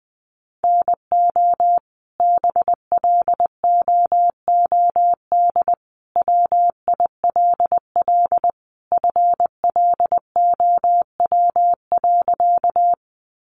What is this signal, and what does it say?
Morse code, heard as a keyed tone, about 20 wpm, 700 Hz. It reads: NO BLOOD WILL FLOW.
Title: Sentences from Top 1200 Words 20wpm